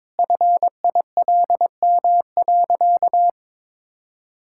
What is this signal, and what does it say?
Morse code, heard as a keyed tone, about 21 wpm, 700 Hz. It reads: FILM.
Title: Sentences from Top 1000 Words - Review Part 5 - 22wpm